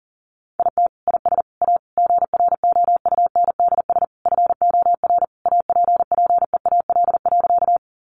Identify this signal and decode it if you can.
Morse code, heard as a keyed tone, about 40 wpm, 700 Hz. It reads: IT IS A GROUNDS FOR APPEAL.
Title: Sentences from Top 1300 Words 40wpm